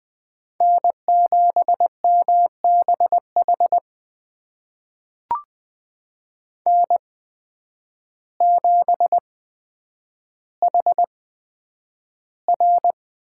Morse code, heard as a keyed tone, about 20 wpm, spaced as for 9 wpm, 700 Hz. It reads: N7MBH N7HR
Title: USA Call Signs - 5 Characters - 8x Character Spacing - 20wpm